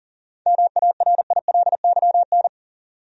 Morse code, heard as a keyed tone, about 40 wpm, 700 Hz. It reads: MARILYN
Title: USA Common Names - 40wpm